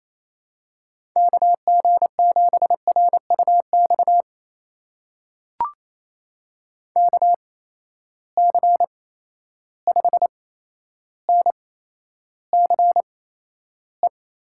Morse code, heard as a keyed tone, about 28 wpm, spaced as for 12 wpm, 700 Hz. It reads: KG7RUX KC5NCE
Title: USA Call Signs - 6 Characters - 8x Character Spacing - 28wpm